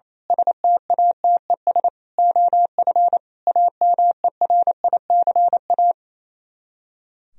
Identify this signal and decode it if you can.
Morse code, heard as a keyed tone, about 27 wpm, 700 Hz. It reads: STATES OF AMERICA
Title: Speed-Racing - Sets of 5 Words 28wpm